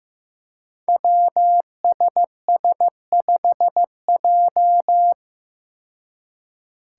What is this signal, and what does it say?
Morse code, heard as a keyed tone, about 15 wpm, 700 Hz. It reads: WSS5J
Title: Five Challenging Letter-Number Combinations 15wpm